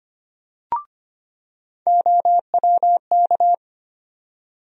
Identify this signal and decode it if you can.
Morse code, heard as a keyed tone, about 25 wpm, 700 Hz. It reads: OWK
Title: USA Call Sign Postfixes - 3 Letters 25wpm